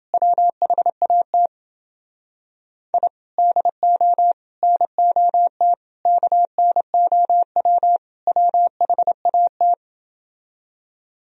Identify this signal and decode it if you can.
Morse code, heard as a keyed tone, about 30 wpm, 700 Hz. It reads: WHAT I DO NOT KNOW WHAT